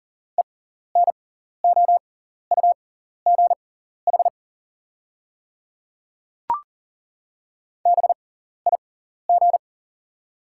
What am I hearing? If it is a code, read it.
Morse code, heard as a keyed tone, about 39 wpm, spaced as for 14 wpm, 690 Hz. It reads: ENOUGH BIG
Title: 6x Character Spacing - Top 500 Words - 40wpm